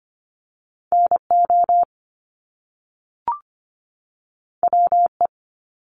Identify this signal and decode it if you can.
Morse code, heard as a keyed tone, about 25 wpm, 700 Hz. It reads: NO WE
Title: Top Two Letter Combinations in Words 25wpm